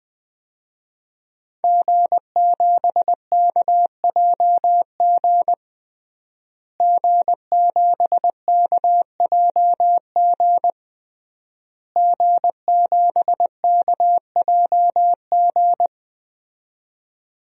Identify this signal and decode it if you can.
Morse code, heard as a keyed tone, about 20 wpm, 700 Hz. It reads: G7KJG G7KJG G7KJG
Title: DXCC-100 Call Signs - 5 Characters - Repeated 3x - 20wpm